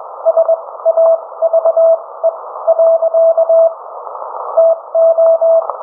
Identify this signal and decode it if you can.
Morse code, heard as a keyed tone, about 21 wpm, spaced as for 16 wpm, 660 Hz. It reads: SAVE. TO